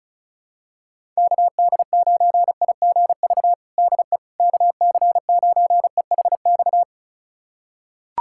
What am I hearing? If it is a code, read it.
Morse code, heard as a keyed tone, about 34 wpm, 700 Hz. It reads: KD9IGV DE KC9EHX
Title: USA Callsign DE Callsign - 6 Characters - 35wpm